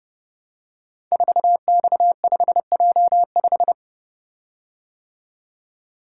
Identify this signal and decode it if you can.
Morse code, heard as a keyed tone, about 28 wpm, 700 Hz. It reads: 4X5J5
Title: Code Groups - 5 Letter:Numbers - 30wpm